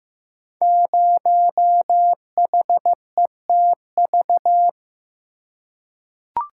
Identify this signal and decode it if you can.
Morse code, heard as a keyed tone, about 15 wpm, 700 Hz. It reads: 0HETV
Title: Code Groups - 5 Letter:Numbers - 15wpm